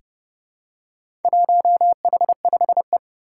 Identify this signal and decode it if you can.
Morse code, heard as a keyed tone, about 29 wpm, 700 Hz. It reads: 1H5E
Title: Four Challenging Letter-Number Combinations 30wpm